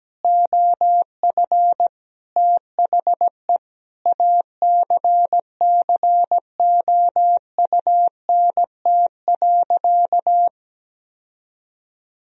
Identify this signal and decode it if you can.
Morse code, heard as a keyed tone, about 17 wpm, 700 Hz. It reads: OF THE ACCOUNT.
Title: Sentences from Top 1000 Words - Review Part 4 - 17wpm